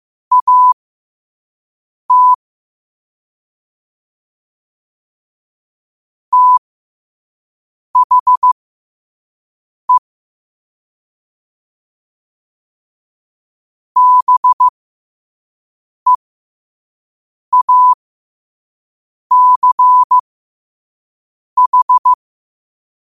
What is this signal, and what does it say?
Morse code, heard as a keyed tone, about 15 wpm, spaced as for 5 wpm, 1000 Hz. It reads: AT THE BEACH
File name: qotdfiles / 5wpm